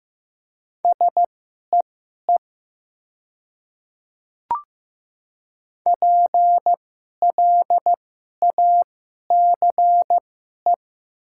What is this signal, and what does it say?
Morse code, heard as a keyed tone, about 15 wpm, spaced as for 11 wpm, 700 Hz. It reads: SEE PLACE
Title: 2x Character Spacing - Top 300 Words - 15wpm